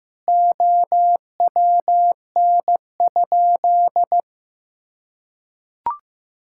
Morse code, heard as a keyed tone, about 15 wpm, 700 Hz. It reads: OWN?
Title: Sentences from Top 1000 Words - Review Part 5 - 15wpm